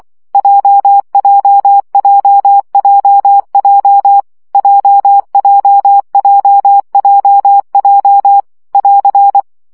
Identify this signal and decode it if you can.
Morse code, heard as a keyed tone, about 24 wpm, 800 Hz. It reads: JJJJJ JJJJJ <AR>
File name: CW gyakorlas j 20 baud